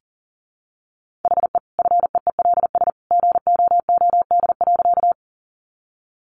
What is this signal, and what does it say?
Morse code, heard as a keyed tone, about 40 wpm, 700 Hz. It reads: HE FEELS GOOD.
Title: Additional Easy Phrases 40wpm